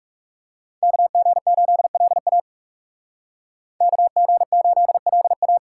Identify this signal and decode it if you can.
Morse code, heard as a keyed tone, about 34 wpm, 700 Hz. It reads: KG8LA KG8LA
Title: Speed Racing - USA Call Signs 30wpm